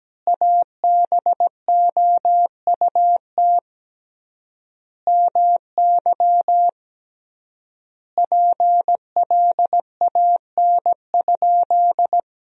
Morse code, heard as a keyed tone, about 17 wpm, 700 Hz. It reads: ABOUT MY PLAN?
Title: 3x Word-Spacing - Sentences from Top 500 Words - 17wpm